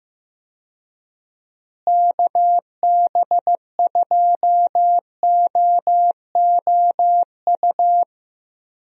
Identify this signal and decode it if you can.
Morse code, heard as a keyed tone, about 15 wpm, 700 Hz. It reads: KB2OOU
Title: USA Call Signs - 6 Characters - Repeated 2x - 15wpm